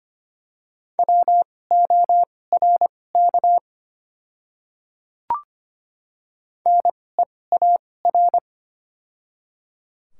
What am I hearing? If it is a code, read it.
Morse code, heard as a keyed tone, about 25 wpm, spaced as for 18 wpm, 690 Hz. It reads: WORK NEAR
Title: 2x Character Spacing - Top 200 Words - 25wpm